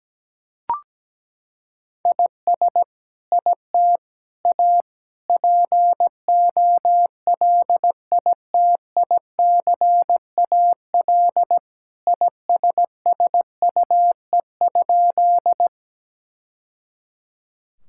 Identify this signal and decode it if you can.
Morse code, heard as a keyed tone, about 17 wpm, 700 Hz. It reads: IS IT A POLITICAL ISSUE?
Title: Sentences from Top 1000 Words - Review Part 6 - 17wpm